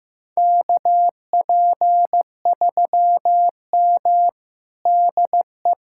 Morse code, heard as a keyed tone, about 15 wpm, 700 Hz. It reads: KP3M DE
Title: USA Callsign DE Callsign - 4 Characters - 15wpm